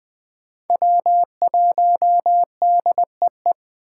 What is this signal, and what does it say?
Morse code, heard as a keyed tone, about 20 wpm, 700 Hz. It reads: W1DEE
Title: USA Call Signs - 5 Characters - 3x Character Spacing - 20wpm